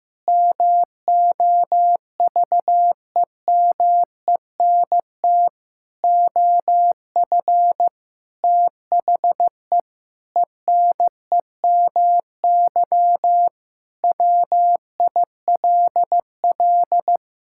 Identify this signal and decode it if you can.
Morse code, heard as a keyed tone, about 16 wpm, 700 Hz. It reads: MOVEMENT OF THE ENEMY WILL
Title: Pangram Challenge v2 - 15wpm